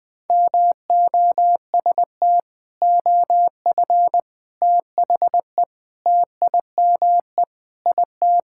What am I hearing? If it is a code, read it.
Morse code, heard as a keyed tone, about 20 wpm, 700 Hz. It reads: MOST OF THE TIME IT
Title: Sentences with 3-Word Phrases - 20wpm